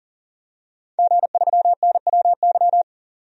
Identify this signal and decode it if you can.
Morse code, heard as a keyed tone, about 40 wpm, 700 Hz. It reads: G3NWY
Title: DXCC-100 Call Signs - 5 Characters - 40wpm